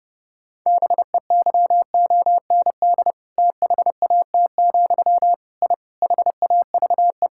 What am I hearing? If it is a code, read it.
Morse code, heard as a keyed tone, about 30 wpm, 700 Hz. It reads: BEYOND THAT, I HAVE
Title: Sentences from Top 1000 Words - Review Part 6 - 30wpm